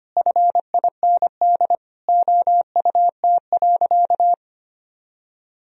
Morse code, heard as a keyed tone, about 25 wpm, 700 Hz. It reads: FIND OUT.
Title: Sentences from Top 1000 Words - Review Part 1 - 25wpm